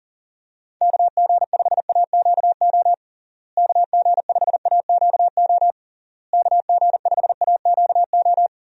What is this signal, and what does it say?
Morse code, heard as a keyed tone, about 40 wpm, 700 Hz. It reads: KG5AQO KG5AQO KG5AQO
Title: USA Call Signs - 6 Characters - Repeated 3x - 40wpm